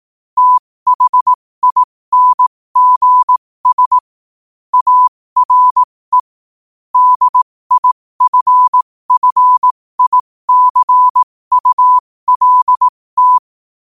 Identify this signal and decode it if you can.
Morse code, heard as a keyed tone, about 18 wpm, spaced as for 15 wpm, 1000 Hz. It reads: THINGS ARE DIFFICULT
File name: qotdfiles / 15wpm